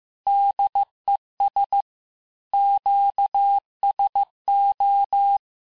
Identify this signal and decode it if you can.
Morse code, heard as a keyed tone, about 15 wpm, 800 Hz. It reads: DES QSO
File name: cours-du-lundi files / cours-1585-texte-FIN-de-QSO-vitesse-14mpm